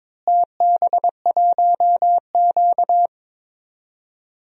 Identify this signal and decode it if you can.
Morse code, heard as a keyed tone, about 22 wpm, 700 Hz. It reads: TB1Q
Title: DXCC-100 Call Signs - 4 Characters - Repeated 2x - 22wpm